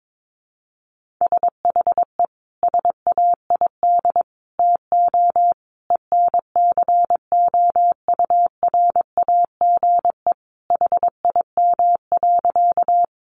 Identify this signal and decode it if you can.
Morse code, heard as a keyed tone, about 22 wpm, 700 Hz. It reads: SHE SAID TO ENCOURAGE HIM.
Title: Sentences from Top 1400 Words 22wpm